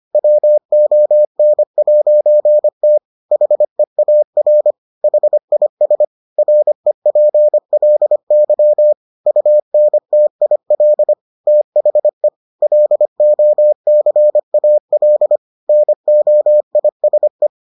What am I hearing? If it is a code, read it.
Morse code, heard as a keyed tone, about 25 wpm, 600 Hz. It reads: WON'T HEAR HIS REPLY UNTIL THE LOCAL NOISE